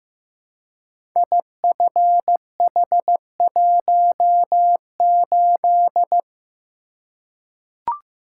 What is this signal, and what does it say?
Morse code, heard as a keyed tone, about 15 wpm, 700 Hz. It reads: IFH18
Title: Code Groups - 5 Letter:Numbers - 15wpm